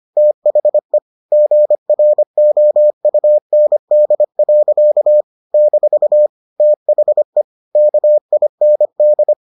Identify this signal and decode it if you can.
Morse code, heard as a keyed tone, about 25 wpm, 600 Hz. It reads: THE GROUND. - THE KIND